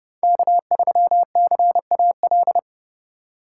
Morse code, heard as a keyed tone, about 30 wpm, 700 Hz. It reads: K3CAL
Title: USA Call Signs - 5 Characters - 2x Character Spacing - 30wpm